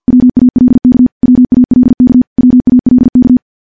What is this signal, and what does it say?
Morse code, heard as a keyed tone, about 50 wpm, 250 Hz. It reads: WALK WALK WALK
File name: morse / morse-training-50WPM